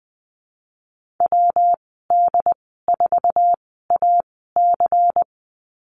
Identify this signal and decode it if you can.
Morse code, heard as a keyed tone, about 20 wpm, spaced as for 14 wpm, 700 Hz. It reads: WD4AC
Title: 154 Lesson 25g - Letter p - Call Signs 20wpm